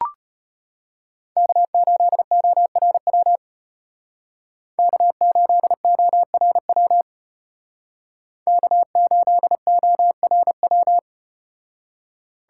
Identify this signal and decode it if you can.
Morse code, heard as a keyed tone, about 33 wpm, 700 Hz. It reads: K8ORW K8ORW K8ORW